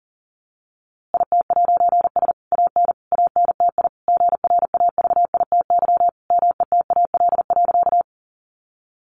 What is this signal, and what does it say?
Morse code, heard as a keyed tone, about 40 wpm, 700 Hz. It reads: IT'S AN ANTI GRAVITY METAL.
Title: A Sound Of Thunder 40wpm